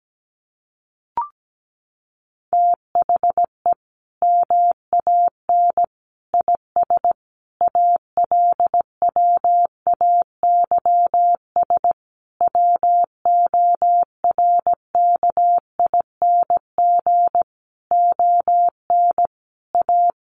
Morse code, heard as a keyed tone, about 17 wpm, 700 Hz. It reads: THE MAN IS ALWAYS WORKING ON A
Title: Sentences from Top 1400 Words 17wpm